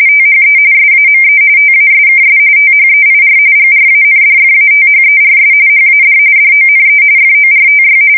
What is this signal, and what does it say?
RTTY, about 45.45 baud. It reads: THE SAME TASKS THEIR GRANDFATHERS HAD DONE FORTY
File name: fsk170 / battleshipnj